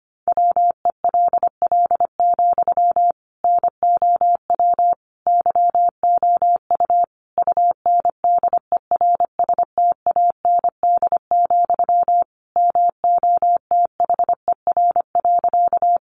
Morse code, heard as a keyed tone, about 25 wpm, 700 Hz. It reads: WELL, NOW YOU UNDERSTAND, MOTHER.
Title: Sentences from Top 600 Words 25wpm